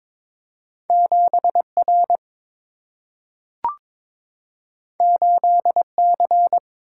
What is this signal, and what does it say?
Morse code, heard as a keyed tone, about 22 wpm, 700 Hz. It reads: 7 R 8 C
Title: Single-Letter Single-Number 22wpm